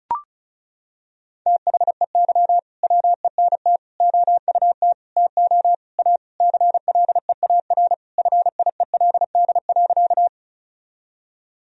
Morse code, heard as a keyed tone, about 36 wpm, 700 Hz. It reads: THEY WENT OUT TO A CLEAR FIELD.
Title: Sentences from Top 1000 Words - Review Part 2 - 35wpm